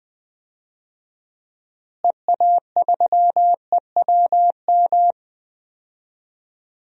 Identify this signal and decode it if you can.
Morse code, heard as a keyed tone, about 20 wpm, 700 Hz. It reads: EA3EWM
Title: DXCC-100 Call Signs - 6 Characters - Repeated 3x - 20wpm